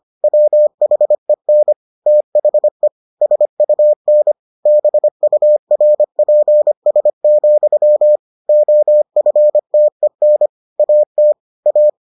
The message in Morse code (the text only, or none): WHEN THE SUN BURPS, OFTEN AT A